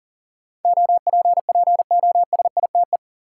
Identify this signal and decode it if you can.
Morse code, heard as a keyed tone, about 40 wpm, 700 Hz. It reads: OPPOSITE
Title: Sentences from Top 1300 Words - New Word Drill - 40wpm